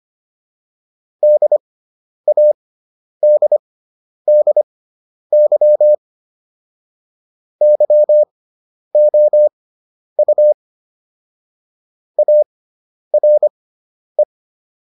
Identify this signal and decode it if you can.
Morse code, heard as a keyed tone, about 25 wpm, spaced as for 10 wpm, 600 Hz. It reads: DADDY YOU ARE